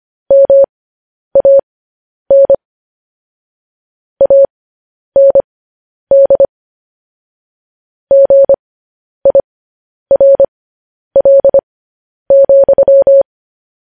Morse code, heard as a keyed tone, about 25 wpm, spaced as for 10 wpm, 550 Hz. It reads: MAN AND GIRL,
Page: news headlines in morse code at 10 wpm